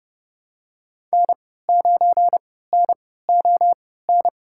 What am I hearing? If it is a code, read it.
Morse code, heard as a keyed tone, about 30 wpm, spaced as for 17 wpm, 700 Hz. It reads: N9NON
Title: USA Call Signs - 3x Character Spacing - 30wpm